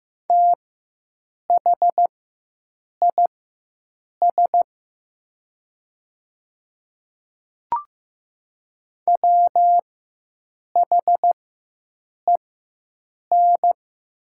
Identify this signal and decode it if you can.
Morse code, heard as a keyed tone, about 15 wpm, spaced as for 7 wpm, 700 Hz. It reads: THIS WHEN
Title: 4x Character Spacing - Top 100 Words - 15wpm